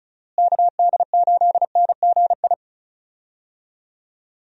Morse code, heard as a keyed tone, about 36 wpm, 700 Hz. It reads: KD8NGI
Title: USA Call Signs - 6 Characters - Repeated 2x - 35wpm